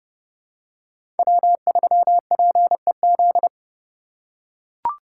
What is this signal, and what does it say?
Morse code, heard as a keyed tone, about 30 wpm, 700 Hz. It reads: W3PEZ
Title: USA Call Signs - 5 Characters - 2x Character Spacing - 30wpm